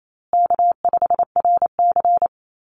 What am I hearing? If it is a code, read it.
Morse code, heard as a keyed tone, about 28 wpm, 700 Hz. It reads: K5RC
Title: Call Signs Repeated 2x - 28wpm